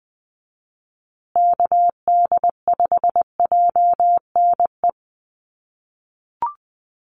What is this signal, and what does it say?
Morse code, heard as a keyed tone, about 20 wpm, 700 Hz. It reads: KD5JNE
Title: USA Call Signs Repeated 3x 20wpm